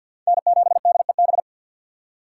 Morse code, heard as a keyed tone, about 51 wpm, 700 Hz. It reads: N7DEB
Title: USA Call Signs - 5 Characters - 2x Character Spacing - 50wpm